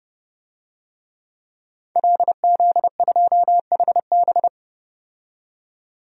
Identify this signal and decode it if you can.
Morse code, heard as a keyed tone, about 31 wpm, 700 Hz. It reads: LZ2HB
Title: DXCC-100 Call Signs - 5 Characters - 30wpm